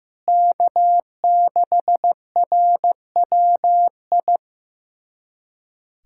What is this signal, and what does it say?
Morse code, heard as a keyed tone, about 15 wpm, 700 Hz. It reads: K6RWI
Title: USA Callsign DE Callsign - 5 Characters - 15wpm